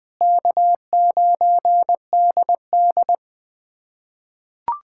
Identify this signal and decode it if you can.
Morse code, heard as a keyed tone, about 21 wpm, 700 Hz. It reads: K9DD
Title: USA Callsign DE Callsign - 4 Characters - 20wpm